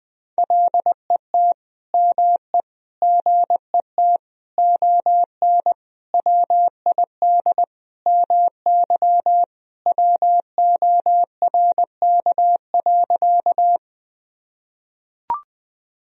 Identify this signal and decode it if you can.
Morse code, heard as a keyed tone, about 20 wpm, 700 Hz. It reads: LET ME GET ON WID MY WORK.
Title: Sentences from Top 300 Words and Abbreviations 20wpm